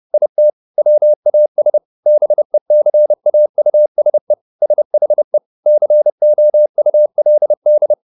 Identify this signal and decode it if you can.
Morse code, heard as a keyed tone, about 30 wpm, 600 Hz. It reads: IT WAS BECAUSE SHE COULD
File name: the old man and the sea / the old man and the sea 1 30wpm